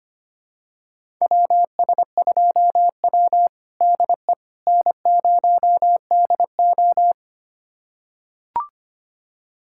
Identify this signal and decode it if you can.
Morse code, heard as a keyed tone, about 25 wpm, 700 Hz. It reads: WS2W DE N0DO
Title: USA Callsign DE Callsign - 4 Characters - 25wpm